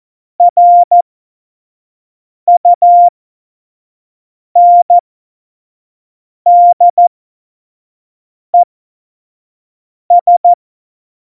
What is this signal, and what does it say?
Morse code, heard as a keyed tone, about 14 wpm, spaced as for 5 wpm, 700 Hz. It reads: RUNDES
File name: Morsekurs / Witze / Witz 25 Juli 17, Tempo 25 F70